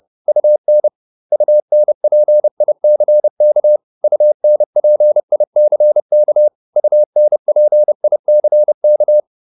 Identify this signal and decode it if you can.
Morse code, heard as a keyed tone, about 30 wpm, 600 Hz. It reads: UN UNPICK UNPICK UNPICK
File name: UN6 30